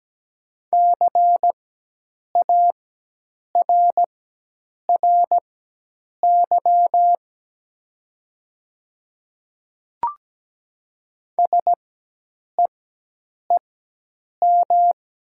Morse code, heard as a keyed tone, about 17 wpm, spaced as for 8 wpm, 700 Hz. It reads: CARRY SEEM
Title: 4x Character Spacing - Top 300 Words - 17wpm